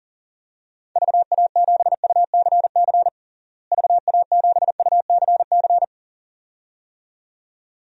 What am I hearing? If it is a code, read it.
Morse code, heard as a keyed tone, about 40 wpm, 700 Hz. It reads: VA7UCC VA7UCC
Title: DXCC-100 Call Signs - 6 Characters - Repeated 2x - 40wpm